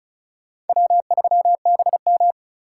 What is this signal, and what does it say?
Morse code, heard as a keyed tone, about 44 wpm, 700 Hz. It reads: W3BM